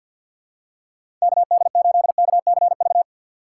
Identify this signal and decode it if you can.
Morse code, heard as a keyed tone, about 51 wpm, 700 Hz. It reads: KD8KCV
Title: USA Call Signs - 6 Characters - 4x Character Spacing - 50wpm